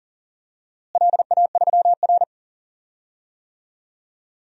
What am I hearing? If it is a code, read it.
Morse code, heard as a keyed tone, about 40 wpm, 700 Hz. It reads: LA3R
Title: DXCC-100 Call Signs - 4 Characters - Repeated 3x - 40wpm